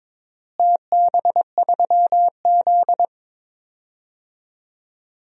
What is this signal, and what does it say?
Morse code, heard as a keyed tone, about 22 wpm, 700 Hz. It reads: TB3Z
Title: DXCC-100 Call Signs - 4 Characters - Repeated 3x - 22wpm